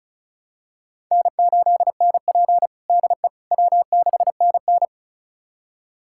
Morse code, heard as a keyed tone, about 36 wpm, 700 Hz. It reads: N8NP DE W6NN
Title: USA Callsign DE Callsign - 4 Characters - 35wpm